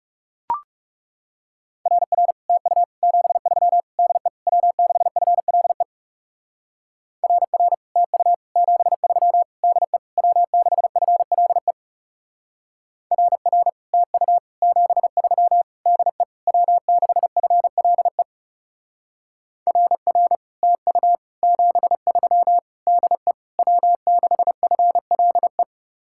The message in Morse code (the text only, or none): RR TU 73 DE W6FLE RR TU 73 DE W6FLE RR TU 73 DE W6FLE RR TU 73 DE W6FLE